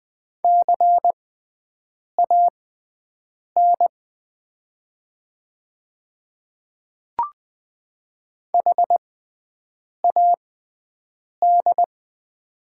Morse code, heard as a keyed tone, about 20 wpm, spaced as for 7 wpm, 690 Hz. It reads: CAN HAD
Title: 6x Character Spacing - Top 100 Words - 20wpm